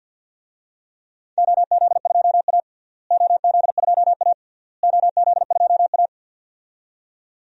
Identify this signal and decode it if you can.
Morse code, heard as a keyed tone, about 53 wpm, 700 Hz. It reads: OZ2A OZ2A OZ2A
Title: DXCC-100 Call Signs - 4 Characters - Repeated 3x - 50wpm